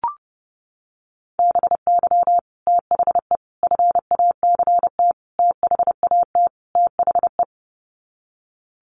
Morse code, heard as a keyed tone, about 30 wpm, 700 Hz. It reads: BY THE FACT THAT THE
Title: Sets of 5 Words Encore 30wpm